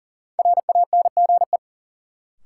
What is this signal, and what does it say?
Morse code, heard as a keyed tone, about 40 wpm, 700 Hz. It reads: RANGE
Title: Sets of 4 Words - New Word Drill - 40wpm